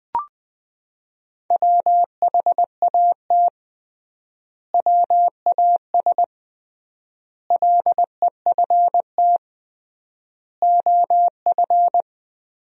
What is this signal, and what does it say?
Morse code, heard as a keyed tone, about 21 wpm, 700 Hz. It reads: WHAT WAS LEFT OF
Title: 3x Word-Spacing - Sentences from Top 300 Words - 20wpm